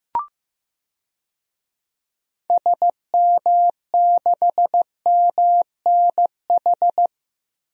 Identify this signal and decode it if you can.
Morse code, heard as a keyed tone, about 15 wpm, 700 Hz. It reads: SM6MNH
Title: DXCC-100 Call Signs - 6 Characters - Repeated 2x - 15wpm